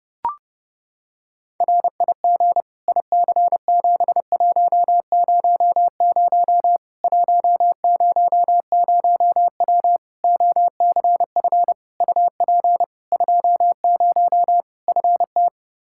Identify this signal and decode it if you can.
Morse code, heard as a keyed tone, about 30 wpm, 700 Hz. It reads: RIG IC7100 100W OCF UP 20 FT
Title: Practice QSO - Exchange 2 - Info - 30wpm